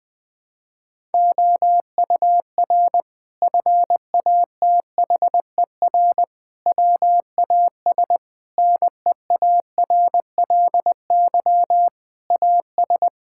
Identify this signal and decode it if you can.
Morse code, heard as a keyed tone, about 21 wpm, 700 Hz. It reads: OUR FATHER WAS NEARLY AS
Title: Sentences from Top 1600 Words 20wpm